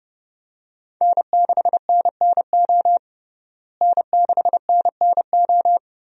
Morse code, heard as a keyed tone, about 29 wpm, 700 Hz. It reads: N6NNO N6NNO
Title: USA Call Signs - 5 Characters - Repeated 3x - 30wpm